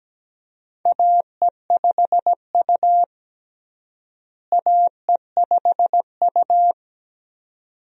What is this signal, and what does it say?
Morse code, heard as a keyed tone, about 17 wpm, 700 Hz. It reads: AE5U AE5U
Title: USA Call Signs - 4 Characters - Repeated 2x - 17wpm